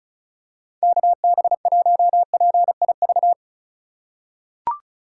Morse code, heard as a keyed tone, about 36 wpm, 700 Hz. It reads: KB1PIV
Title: USA Call Signs - 6 Characters - 3x Character Spacing - 35wpm